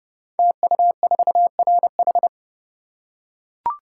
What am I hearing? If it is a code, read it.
Morse code, heard as a keyed tone, about 31 wpm, 700 Hz. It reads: TU4RH
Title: Code Groups - 5 Letter:Numbers - 30wpm